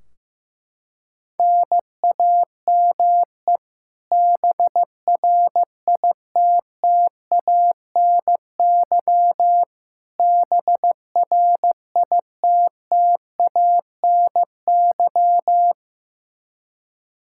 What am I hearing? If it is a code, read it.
Morse code, heard as a keyed tone, about 15 wpm, 700 Hz. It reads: NAME BRITTANY BRITTANY
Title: Practice QSO - Exchange 1 - Beginning - 15wpm